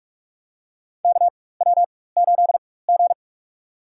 Morse code, heard as a keyed tone, about 46 wpm, spaced as for 21 wpm, 700 Hz. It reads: KW8G
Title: USA Call Signs - 4x Character Spacing - 45wpm